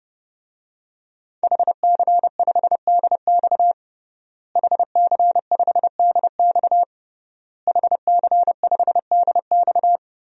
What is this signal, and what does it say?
Morse code, heard as a keyed tone, about 30 wpm, 700 Hz. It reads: HC5DX HC5DX HC5DX
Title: DXCC-100 Call Signs - 5 Characters - Repeated 3x - 30wpm